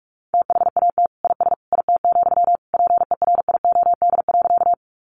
Morse code, heard as a keyed tone, about 44 wpm, 700 Hz. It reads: THAT IS IT, PERIOD.
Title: Sentences from Top 800 Words 45wpm